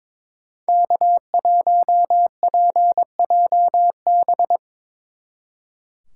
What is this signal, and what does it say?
Morse code, heard as a keyed tone, about 25 wpm, 700 Hz. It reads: K1PJB
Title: Speed Racing - USA 1x3 Call Signs 22wpm